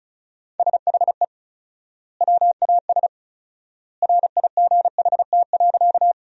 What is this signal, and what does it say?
Morse code, heard as a keyed tone, about 34 wpm, 700 Hz. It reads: SHE WAS RIGHT.
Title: 4x Word-Spacing - Sentences from Top 200 Words - 35wpm